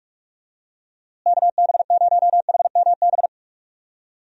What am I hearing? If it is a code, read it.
Morse code, heard as a keyed tone, about 46 wpm, 700 Hz. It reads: KB0HMB
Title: USA Call Signs - 6 Characters - Repeated 2x - 45wpm